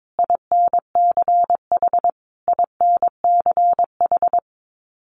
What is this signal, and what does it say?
Morse code, heard as a keyed tone, about 22 wpm, 700 Hz. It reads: INCH INCH
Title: Top 500 Words Repeated 22wpm